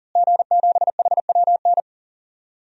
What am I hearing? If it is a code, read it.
Morse code, heard as a keyed tone, about 40 wpm, 700 Hz. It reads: G7HWN
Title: DXCC-100 Call Signs - 5 Characters - Repeated 2x - 40wpm